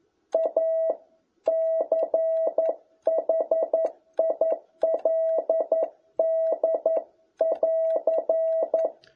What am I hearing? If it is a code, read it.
Morse code, heard as a keyed tone, about 11 wpm, 650 Hz. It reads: A CHILD<AR>
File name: jh8cbh / 2021cw